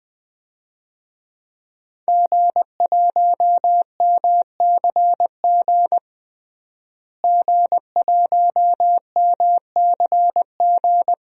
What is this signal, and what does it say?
Morse code, heard as a keyed tone, about 21 wpm, 700 Hz. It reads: G1MCG G1MCG
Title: DXCC-100 Call Signs - 5 Characters - Repeated 3x - 20wpm